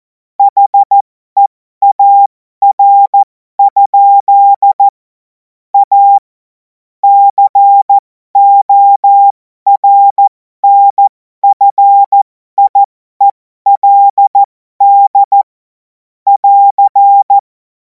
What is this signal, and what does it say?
Morse code, heard as a keyed tone, about 14 wpm, spaced as for 12 wpm, 800 Hz. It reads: HEAR? A CORNFIELD <AR>
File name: Morsekurs / Witze